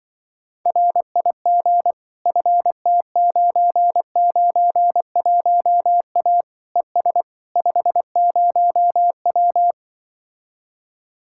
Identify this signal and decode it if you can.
Morse code, heard as a keyed tone, about 25 wpm, 700 Hz. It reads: RIG FT991A ES 50W
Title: Speed-Racing Common QSO Phrases - 20wpm